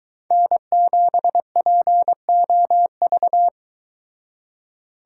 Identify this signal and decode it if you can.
Morse code, heard as a keyed tone, about 22 wpm, 700 Hz. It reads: N7POV